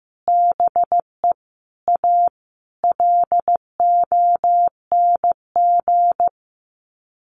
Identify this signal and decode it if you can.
Morse code, heard as a keyed tone, about 15 wpm, 700 Hz. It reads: BE A LONG
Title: Sets of 5 Words Encore2 15wpm